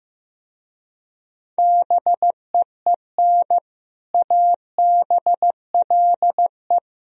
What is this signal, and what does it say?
Morse code, heard as a keyed tone, about 17 wpm, 700 Hz. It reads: BEEN ABLE
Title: Speed-Racing - Sets of 3 Words 15wpm